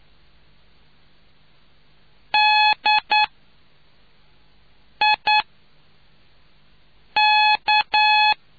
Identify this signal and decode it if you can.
Morse code, heard as a keyed tone, about 9 wpm, 850 Hz. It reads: D I K